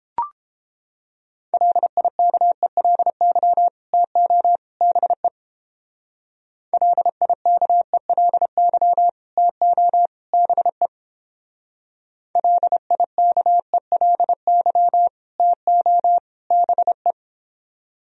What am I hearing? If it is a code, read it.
Morse code, heard as a keyed tone, about 29 wpm, 700 Hz. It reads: LIKELY TO BE LIKELY TO BE LIKELY TO BE